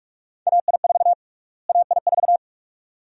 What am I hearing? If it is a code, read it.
Morse code, heard as a keyed tone, about 46 wpm, 700 Hz. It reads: AI4 AI4
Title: USA Call Sign Prefixes - Repeated 2x - 45wpm